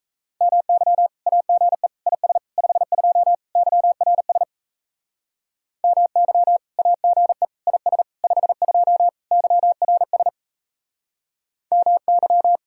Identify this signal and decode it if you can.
Morse code, heard as a keyed tone, about 35 wpm, 700 Hz. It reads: MY AGE IS 52 YRS MY AGE IS 52 YRS MY
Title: Speed-Racing Common QSO Phrases Encore - 28wpm